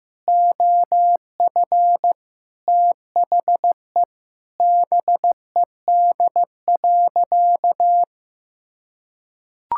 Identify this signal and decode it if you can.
Morse code, heard as a keyed tone, about 15 wpm, 700 Hz. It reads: OF THE BED.
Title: Sentences from Top 1000 Words - Review Part 6 - 15wpm